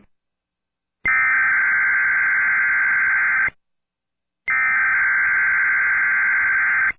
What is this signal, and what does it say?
Emergency Alert System (SAME) tones (header bursts), heard as harsh buzzing bursts. It is heard from 1.1 to 5.9 s.